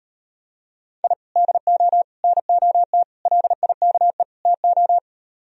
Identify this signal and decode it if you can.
Morse code, heard as a keyed tone, about 38 wpm, 700 Hz. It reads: I DO NOT LIKE TO